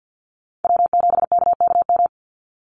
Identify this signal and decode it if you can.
Morse code, heard as a keyed tone, about 50 wpm, 700 Hz. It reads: W7KKM
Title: NA SOTA Chaser Callsigns 50wpm